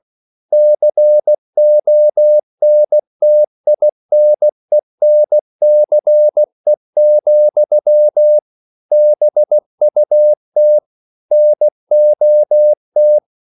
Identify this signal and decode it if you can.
Morse code, heard as a keyed tone, about 16 wpm, 600 Hz. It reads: CONTINENCE, BUT NOT
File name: CW / Practice / EnglishQuotations0116